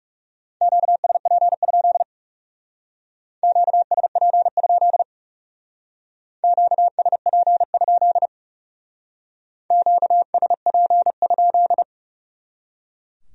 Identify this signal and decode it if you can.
Morse code, heard as a keyed tone, about 36 wpm, 700 Hz. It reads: QSP? QSP? QSP? QSP?
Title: Speed-Racing Q-Codes - 30wpm